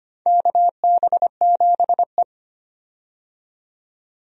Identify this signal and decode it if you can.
Morse code, heard as a keyed tone, about 25 wpm, 700 Hz. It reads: KB7E
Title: USA Call Signs - 4 Characters - 25wpm